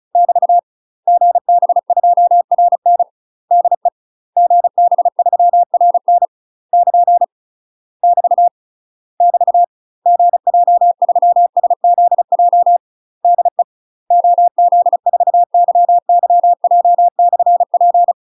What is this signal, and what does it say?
Morse code, heard as a keyed tone, about 36 wpm, 700 Hz. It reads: <BT> GB2RN DE GB3RN <KN> <BT> <BT> GJ3SZJ DE OZ4YYJ/P
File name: UK MASTER QSOs TRAINING / 35WPM / UK MASTER QSOs 0016